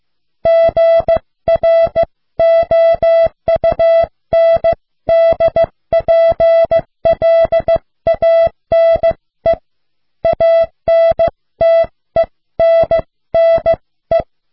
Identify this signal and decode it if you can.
Morse code, heard as a keyed tone, about 15 wpm, 660 Hz. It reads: GROUNDPLANE ANTENNE